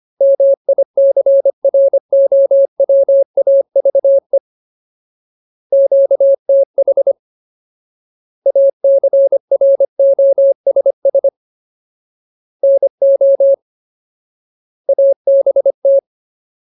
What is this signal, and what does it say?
Morse code, heard as a keyed tone, about 25 wpm, 550 Hz. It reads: MICROWAVE QTH ACROSS NO ABT
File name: Ultimate Word List / 25WPM Ultimate